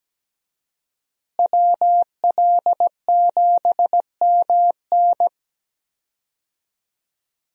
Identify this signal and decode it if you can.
Morse code, heard as a keyed tone, about 17 wpm, 700 Hz. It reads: WL7MN
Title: USA Call Signs - 5 Characters - Repeated 3x - 17wpm